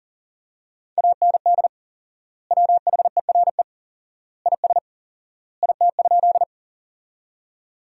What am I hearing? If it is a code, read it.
Morse code, heard as a keyed tone, about 40 wpm, 700 Hz. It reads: AND WHERE IS IT?
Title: 4x Word-Spacing - Sentences from Top 200 Words - 40wpm